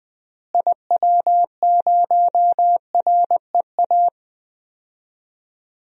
Morse code, heard as a keyed tone, about 20 wpm, 700 Hz. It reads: IW0REA
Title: DXCC-100 Call Signs - 6 Characters - Repeated 2x - 20wpm